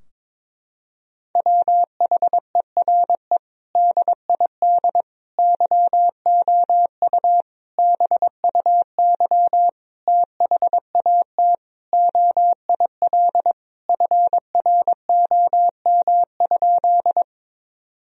Morse code, heard as a keyed tone, about 22 wpm, 700 Hz. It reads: WHERE DID YOU BUY THAT OIL FROM?
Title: Sentences from Top 1000 Words - Review Part 4 - 22wpm